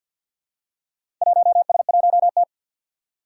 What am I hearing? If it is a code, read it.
Morse code, heard as a keyed tone, about 51 wpm, 700 Hz. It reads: 1S1T
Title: Four Challenging Letter-Number Combinations 50wpm